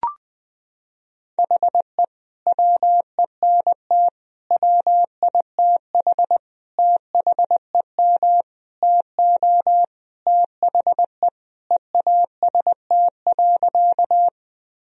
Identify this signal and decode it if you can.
Morse code, heard as a keyed tone, about 20 wpm, 700 Hz. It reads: HE WENT WITH THEM TO THE EAST.
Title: Sentences from Top 1000 Words - Review Part 3 - 20wpm